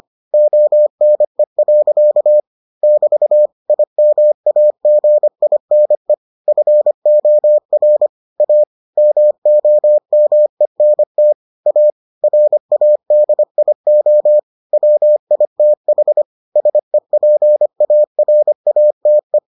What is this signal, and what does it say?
Morse code, heard as a keyed tone, about 25 wpm, 600 Hz. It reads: ONE. <BT> IMAGINE FOR A MOMENT A RADIO WITH SEPARATE